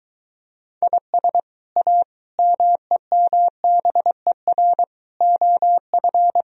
Morse code, heard as a keyed tone, about 24 wpm, 700 Hz. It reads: IS A MEMBER OF